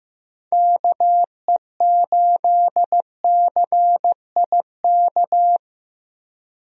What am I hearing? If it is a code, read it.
Morse code, heard as a keyed tone, about 15 wpm, 700 Hz. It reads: KE8CIK
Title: USA Call Signs - 6 Characters - Repeated 3x - 15wpm